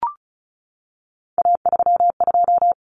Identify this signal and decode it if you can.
Morse code, heard as a keyed tone, about 34 wpm, 700 Hz. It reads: A32
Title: Common Cut Numbers - 3-Digit 35wpm